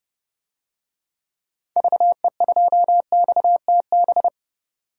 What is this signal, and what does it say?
Morse code, heard as a keyed tone, about 30 wpm, 700 Hz. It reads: VE2XTB
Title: DXCC-100 Call Signs - 6 Characters - 30wpm